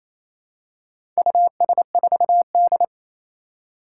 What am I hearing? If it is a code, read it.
Morse code, heard as a keyed tone, about 28 wpm, 700 Hz. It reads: US4D
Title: Four Challenging Letter-Number Combinations 28wpm